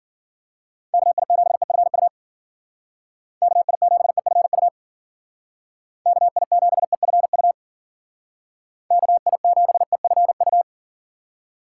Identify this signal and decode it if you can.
Morse code, heard as a keyed tone, about 49 wpm, 700 Hz. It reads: KI7EFU KI7EFU KI7EFU KI7EFU
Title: Speed Racing - USA 2x3 Call Signs 40wpm